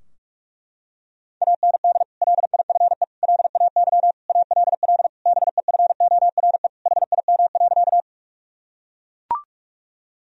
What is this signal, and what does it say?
Morse code, heard as a keyed tone, about 41 wpm, 700 Hz. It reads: AND LIFE LAY ALL BEFORE HIM.
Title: Sentences from Top 1000 Words - Review Part 2 - 45wpm